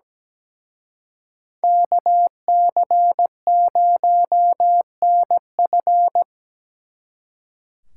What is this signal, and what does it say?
Morse code, heard as a keyed tone, about 20 wpm, 700 Hz. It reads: KC0NF
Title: Speed Racing - USA 2x2 Call Signs 17wpm